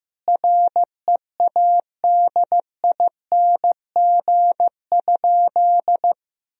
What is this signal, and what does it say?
Morse code, heard as a keyed tone, about 15 wpm, 700 Hz. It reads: READING?
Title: Sentences from Top 1000 Words - Review Part 3 - 15wpm